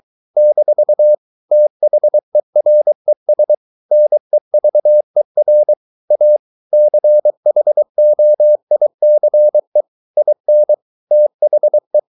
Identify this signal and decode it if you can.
Morse code, heard as a keyed tone, about 23 wpm, 600 Hz. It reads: - THERES NEVER A CHOICE IN THE